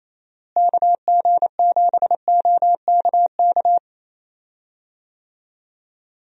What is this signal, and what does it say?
Morse code, heard as a keyed tone, about 28 wpm, 700 Hz. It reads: KG7OKK
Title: USA Call Signs - 6 Characters - Repeated 2x - 28wpm